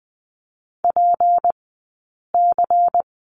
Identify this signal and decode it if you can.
Morse code, heard as a keyed tone, about 20 wpm, 700 Hz. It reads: P C
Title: 177 Lesson 29d - Slash - Two Characters 20wpm